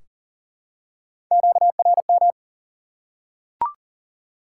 Morse code, heard as a keyed tone, about 40 wpm, 700 Hz. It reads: QRM
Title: Top 90 QSO Elements 40wpm